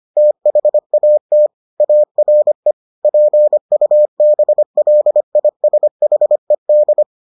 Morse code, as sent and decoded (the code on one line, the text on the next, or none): - .... .- - / .- .-. / .--. ..- -... .-.. .. ... .... -..
THAT ARE PUBLISHED